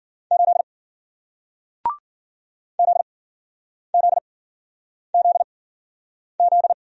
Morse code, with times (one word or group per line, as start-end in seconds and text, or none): 0.3-0.6 8
2.8-6.7 ZZZZ